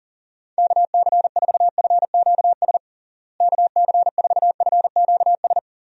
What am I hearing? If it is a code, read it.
Morse code, heard as a keyed tone, about 40 wpm, 700 Hz. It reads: KC4FQS KC4FQS
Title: USA Call Signs - 6 Characters - Repeated 3x - 40wpm